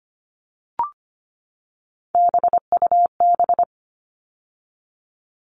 Morse code, heard as a keyed tone, about 25 wpm, 700 Hz. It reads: BUB
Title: All Cut Numbers - 3-Digit 25wpm